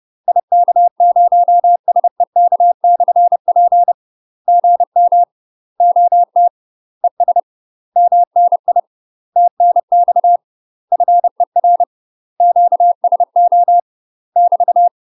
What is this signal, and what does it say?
Morse code, heard as a keyed tone, about 30 wpm, 700 Hz. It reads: IK0SEK/P GM OT ES MNI TNX FER QSO <BT>
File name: UK MASTER QSOs TRAINING / 30WPM / UK MASTER QSOs 0021